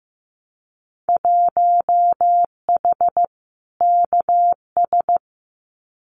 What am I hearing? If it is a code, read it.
Morse code, heard as a keyed tone, about 15 wpm, 700 Hz. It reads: 1H KS
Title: Winter Field Day Exchanges 15wpm